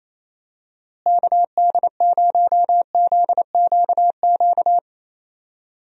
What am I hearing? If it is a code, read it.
Morse code, heard as a keyed tone, about 28 wpm, 700 Hz. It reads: KD0ZQQ
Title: USA Call Signs - 6 Characters - Repeated 2x - 28wpm